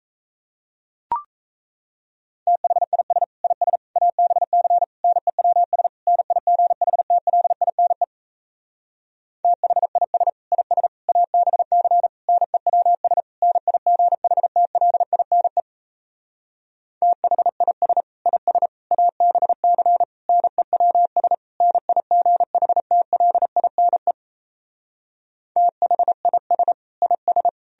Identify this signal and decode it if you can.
Morse code, heard as a keyed tone, about 38 wpm, 700 Hz. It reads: THIS IS ABC NEWS NIGHTLINE THIS IS ABC NEWS NIGHTLINE THIS IS ABC NEWS NIGHTLINE THIS IS